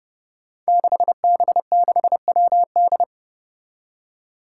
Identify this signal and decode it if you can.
Morse code, heard as a keyed tone, about 30 wpm, 700 Hz. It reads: 6B6WD
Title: Five Challenging Letter-Number Combinations 30wpm